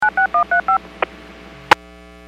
DTMF, heard as telephone touch tones, tones of about 110 ms. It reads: 96135